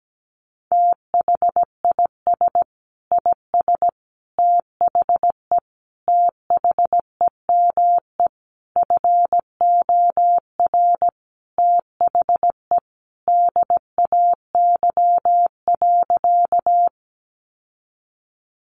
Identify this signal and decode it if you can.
Morse code, heard as a keyed tone, about 17 wpm, 700 Hz. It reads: THIS IS THE THEME FOR THE DAY.
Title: Sentences from Top 1100 Words 17wpm